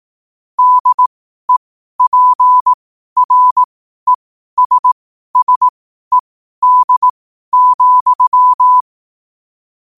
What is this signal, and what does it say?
Morse code, heard as a keyed tone, about 18 wpm, spaced as for 12 wpm, 1000 Hz. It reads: DEPRESSED,
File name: qotdfiles / 12wpm